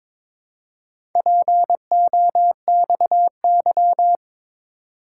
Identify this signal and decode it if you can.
Morse code, heard as a keyed tone, about 22 wpm, 700 Hz. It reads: POXY
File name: practice / 22wpm / Four Letter-Number Combinations - 22wpm